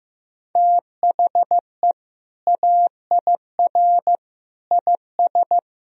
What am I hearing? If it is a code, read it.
Morse code, heard as a keyed tone, about 15 wpm, 700 Hz. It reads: THE AIR IS
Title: Sentences from Top 1000 Words - Review Part 1 - 15wpm